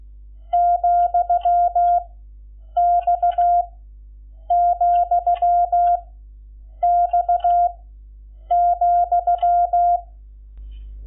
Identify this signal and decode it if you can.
Morse code, heard as a keyed tone, about 16 wpm, 700 Hz. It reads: , X , X ,
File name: Lecciones / CLASE Nº11